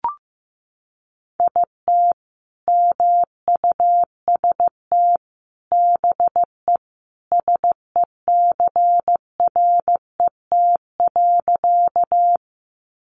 Morse code, heard as a keyed tone, about 15 wpm, 700 Hz. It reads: IT MUST BE SECRET.
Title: Sentences from Top 1200 Words 15wpm